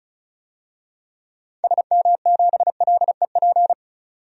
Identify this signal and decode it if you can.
Morse code, heard as a keyed tone, about 36 wpm, 700 Hz. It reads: SM7LEP
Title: DXCC-100 Call Signs - 6 Characters - 35wpm